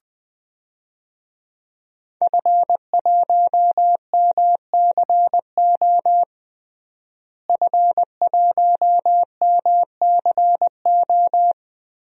Morse code, heard as a keyed tone, about 20 wpm, 700 Hz. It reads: F1MCO F1MCO
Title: DXCC-100 Call Signs - 5 Characters - Repeated 2x - 20wpm